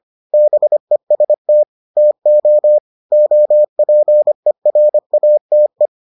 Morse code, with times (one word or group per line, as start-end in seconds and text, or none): 0.3-1.6 BEST
2.0-2.8 TO
3.1-5.9 OPERATE